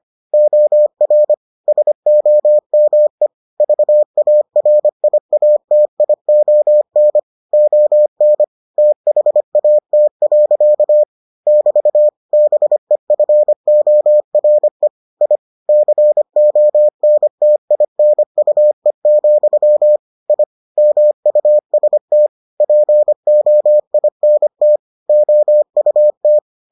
Morse code, heard as a keyed tone, about 25 wpm, 600 Hz. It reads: OR SOME VARIATION ON THAT. <BT> BEFORE I CONTINUE, I MUST POINT OUT